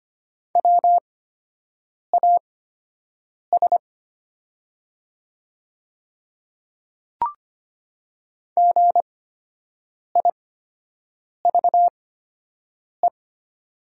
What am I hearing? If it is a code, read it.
Morse code, heard as a keyed tone, about 25 wpm, spaced as for 7 wpm, 700 Hz. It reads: WAS GIVE
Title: 8x Character Spacing - Top 200 Words - 25wpm